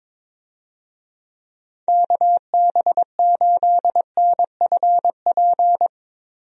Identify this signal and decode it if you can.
Morse code, heard as a keyed tone, about 22 wpm, 700 Hz. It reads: KB8NFP
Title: USA Call Signs - 6 Characters - Repeated 2x - 22wpm